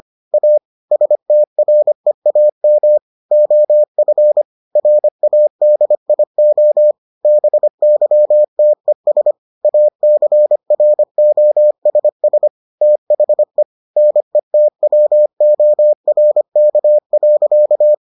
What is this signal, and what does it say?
Morse code, heard as a keyed tone, about 25 wpm, 600 Hz. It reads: A STREAM OF RADIO BYTES ACROSS THE NETWORK.